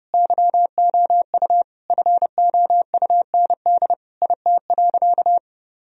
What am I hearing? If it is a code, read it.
Morse code, heard as a keyed tone, about 30 wpm, 700 Hz. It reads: YOU FOUND IT.
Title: Sentences from Top 1000 Words - Review Part 4 - 30wpm